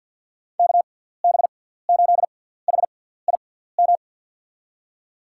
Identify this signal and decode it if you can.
Morse code, heard as a keyed tone, about 52 wpm, spaced as for 17 wpm, 700 Hz. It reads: KB8HIM
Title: USA Call Signs - 6 Characters - 6x Character Spacing - 50wpm